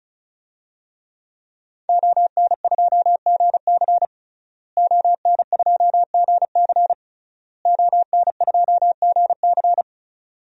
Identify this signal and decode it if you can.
Morse code, heard as a keyed tone, about 36 wpm, 700 Hz. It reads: ON2GC ON2GC ON2GC
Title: DXCC-100 Call Signs - 5 Characters - Repeated 3x - 35wpm